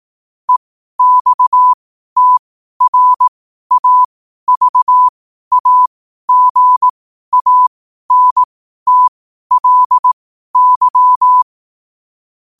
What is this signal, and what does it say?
Morse code, heard as a keyed tone, about 18 wpm, spaced as for 12 wpm, 1000 Hz. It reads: EXTRAVAGANTLY
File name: qotdfiles / 12wpm